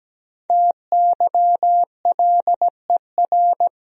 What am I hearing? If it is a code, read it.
Morse code, heard as a keyed tone, about 17 wpm, 700 Hz. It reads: TYLER
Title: USA Common Names - Max 5 Letters - 17wpm